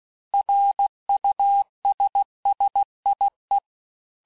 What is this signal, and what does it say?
Morse code, heard as a keyed tone, about 16 wpm, 800 Hz. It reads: RUSSIE
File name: texte-clair-vitesse-15mpm-puis-20mpm